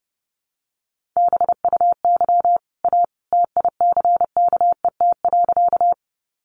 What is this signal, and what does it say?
Morse code, heard as a keyed tone, about 30 wpm, 700 Hz. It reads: BUY A TICKET.
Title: Sentences from Top 1400 Words 30wpm